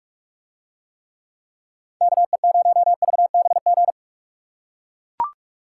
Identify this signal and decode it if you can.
Morse code, heard as a keyed tone, about 44 wpm, 700 Hz. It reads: KE0VBG